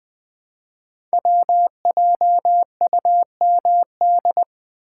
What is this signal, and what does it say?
Morse code, heard as a keyed tone, about 20 wpm, 700 Hz. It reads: WJUMD
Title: Code Groups - 5 Letters - 20wpm